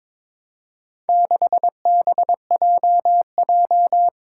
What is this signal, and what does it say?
Morse code, heard as a keyed tone, about 22 wpm, 700 Hz. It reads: 6BJJ
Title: Four Challenging Letter-Number Combinations 22wpm